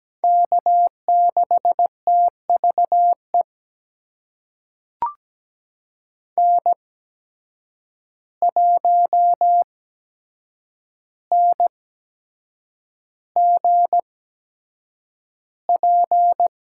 Morse code, heard as a keyed tone, about 17 wpm, spaced as for 7 wpm, 700 Hz. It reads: K6TVE N1NGP
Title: USA Call Signs - 5 Characters - 8x Character Spacing - 17wpm